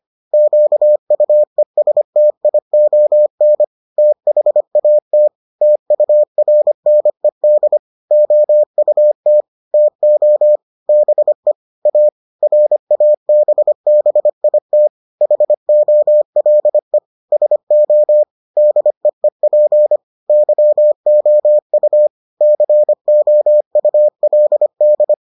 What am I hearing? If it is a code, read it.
Morse code, heard as a keyed tone, about 25 wpm, 600 Hz. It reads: QUESTION THAT TURNED OUT TO BE A RABBIT HOLE SO DEEP YOU COULD